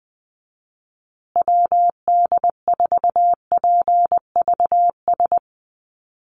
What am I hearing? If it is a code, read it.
Morse code, heard as a keyed tone, about 20 wpm, 700 Hz. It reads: WD4PVS
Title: USA Call Signs Repeated 2x 20wpm